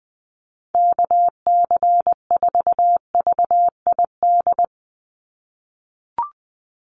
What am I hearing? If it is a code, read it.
Morse code, heard as a keyed tone, about 20 wpm, 700 Hz. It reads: KC4VID
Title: USA Call Signs 20wpm